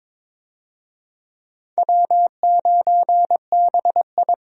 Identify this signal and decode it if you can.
Morse code, heard as a keyed tone, about 22 wpm, 700 Hz. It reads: W9BI
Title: USA Call Signs - 4 Characters - 22wpm